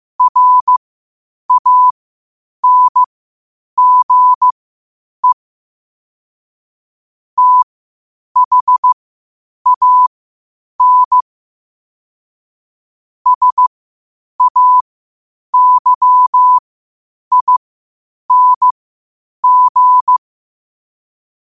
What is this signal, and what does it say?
Morse code, heard as a keyed tone, about 15 wpm, spaced as for 8 wpm, 1000 Hz. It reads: RANGE THAN SAYING